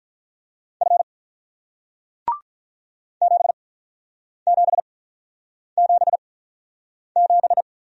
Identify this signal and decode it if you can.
Morse code, heard as a keyed tone, about 48 wpm, spaced as for 9 wpm, 700 Hz. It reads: F 7777